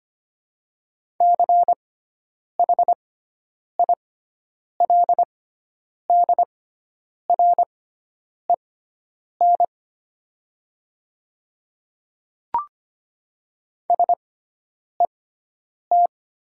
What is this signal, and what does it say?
Morse code, heard as a keyed tone, about 25 wpm, spaced as for 9 wpm, 700 Hz. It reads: CHILDREN SET
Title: 6x Character Spacing - Top 300 Words - 25wpm